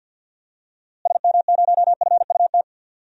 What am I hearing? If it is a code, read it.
Morse code, heard as a keyed tone, about 47 wpm, 700 Hz. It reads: SM0FUT
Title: DXCC-100 Call Signs - 6 Characters - Repeated 2x - 50wpm